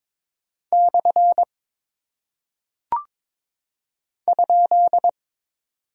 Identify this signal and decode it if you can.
Morse code, heard as a keyed tone, about 22 wpm, 700 Hz. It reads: / ?
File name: practice / 22wpm / Single Punctuation-Prosign - 22wpm